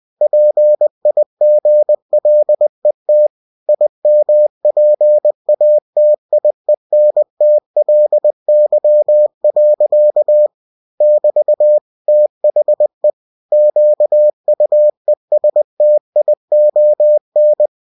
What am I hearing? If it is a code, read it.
Morse code, heard as a keyed tone, about 20 wpm, 600 Hz. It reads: PIGLET IMPATIENTLY. <BT> THE QUESTION
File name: cw-books / winnie-the-pooh / winnie-20wpm-page-0061